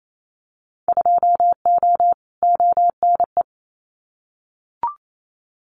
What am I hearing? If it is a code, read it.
Morse code, heard as a keyed tone, about 28 wpm, 700 Hz. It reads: 2O ONE
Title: Winter Field Day Exchanges Encore 28wpm